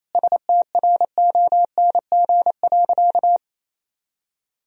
Morse code, heard as a keyed tone, about 27 wpm, 700 Hz. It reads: STRONG.
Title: Sentences from Top 1000 Words - Review Part 2 - 28wpm